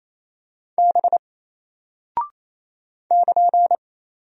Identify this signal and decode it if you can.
Morse code, heard as a keyed tone, about 27 wpm, 700 Hz. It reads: B <KN>
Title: Single Character (Letter:Number:Punctuation:Prosign) Alt 28wpm